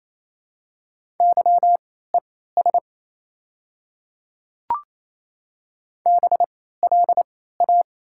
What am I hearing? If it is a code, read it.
Morse code, heard as a keyed tone, about 27 wpm, spaced as for 16 wpm, 700 Hz. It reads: YES BLA
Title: 3x Character Spacing - Top 500 Words - 28wpm